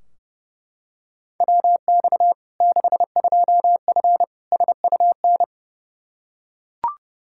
Morse code, heard as a keyed tone, about 30 wpm, 700 Hz. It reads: WX 62F SUN
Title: Practice QSO - Exchange 2 - Info - 30wpm